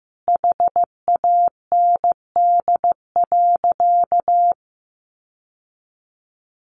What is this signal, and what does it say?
Morse code, heard as a keyed tone, about 15 wpm, 700 Hz. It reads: HAND.
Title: Sentences from Top 1200 Words 15wpm